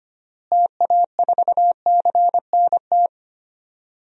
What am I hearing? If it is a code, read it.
Morse code, heard as a keyed tone, about 25 wpm, 700 Hz. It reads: TA4CNT
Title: DXCC-100 Call Signs - 6 Characters - Repeated 2x - 25wpm